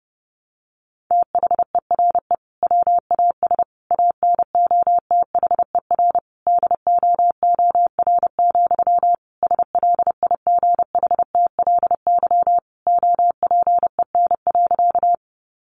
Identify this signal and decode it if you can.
Morse code, heard as a keyed tone, about 30 wpm, 700 Hz. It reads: THERE WAS ANOTHER DOOR, SLIGHTLY OPEN.
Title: Sentences from Top 1300 Words 30wpm